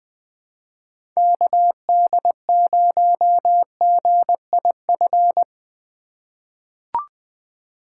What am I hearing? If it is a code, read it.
Morse code, heard as a keyed tone, about 21 wpm, 700 Hz. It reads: KD0GIF
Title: USA Call Signs - 6 Characters - 3x Character Spacing - 20wpm